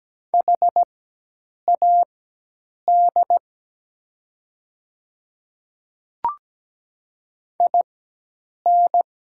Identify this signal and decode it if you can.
Morse code, heard as a keyed tone, about 17 wpm, spaced as for 8 wpm, 700 Hz. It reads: HAD IN
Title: 4x Character Spacing - Top 100 Words - 17wpm